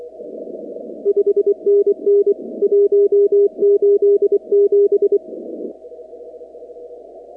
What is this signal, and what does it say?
Morse code, heard as a keyed tone, about 24 wpm, 400 Hz. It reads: 5NN 187
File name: contests / 2019-cw-wpx-kt5j-k5pi- -2330 0-44